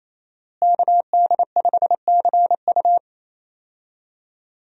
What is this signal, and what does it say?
Morse code, heard as a keyed tone, about 28 wpm, 700 Hz. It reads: KD5CU
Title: USA Call Signs - 5 Characters - Repeated 3x - 28wpm